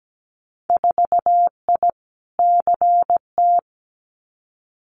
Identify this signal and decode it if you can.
Morse code, heard as a keyed tone, about 17 wpm, 700 Hz. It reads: 4I CT
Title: Winter Field Day Exchanges Encore 17wpm